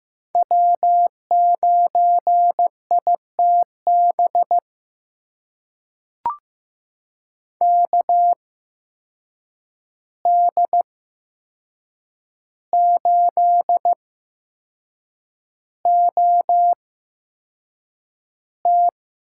Morse code, heard as a keyed tone, about 15 wpm, spaced as for 6 wpm, 700 Hz. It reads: W9ITB KD8OT
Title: USA Call Signs - 5 Characters - 8x Character Spacing - 15wpm